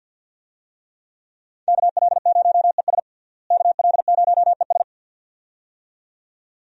Morse code, heard as a keyed tone, about 52 wpm, 700 Hz. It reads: KL0ES KL0ES
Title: USA Call Signs - 5 Characters - Repeated 2x - 50wpm